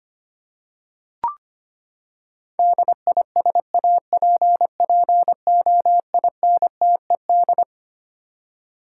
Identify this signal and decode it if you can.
Morse code, heard as a keyed tone, about 25 wpm, 700 Hz. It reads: DISAPPOINTED
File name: Twelve Letter Words - 25wpm